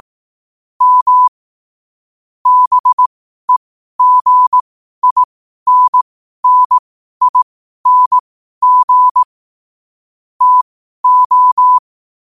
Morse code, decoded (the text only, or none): M BEGINNING TO